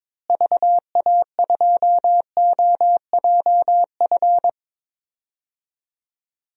Morse code, heard as a keyed tone, about 22 wpm, 700 Hz. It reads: VA2OJF
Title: DXCC-100 Call Signs - 6 Characters - Repeated 3x - 22wpm